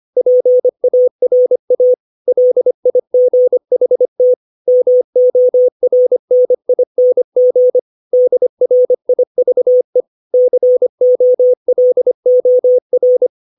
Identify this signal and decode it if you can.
Morse code, heard as a keyed tone, about 25 wpm, 500 Hz. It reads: PARA LIGHT MORNING DRIVE COLOR
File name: 632 words / 632 Words 25WPM